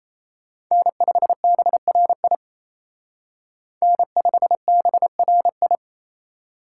Morse code, heard as a keyed tone, about 30 wpm, 700 Hz. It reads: N5BRI N5BRI
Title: Speed Racing - USA 1x3 Call Signs 28wpm